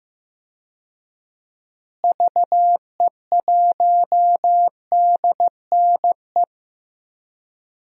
Morse code, heard as a keyed tone, about 15 wpm, 700 Hz. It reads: VE1DNE
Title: DXCC-100 Call Signs - 6 Characters - Repeated 2x - 15wpm